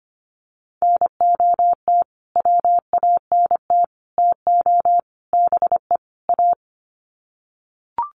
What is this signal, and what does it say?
Morse code, heard as a keyed tone, about 25 wpm, 700 Hz. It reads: NOT WANT TO BE A
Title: Sets of 5 Words Encore 25wpm